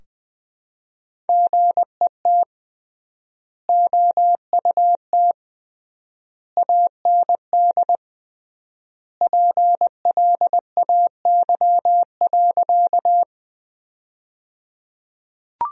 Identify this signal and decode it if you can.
Morse code, heard as a keyed tone, about 20 wpm, 700 Hz. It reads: GET OUT AND PLAY.
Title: 3x Word-Spacing - Sentences from Top 200 Words - 20wpm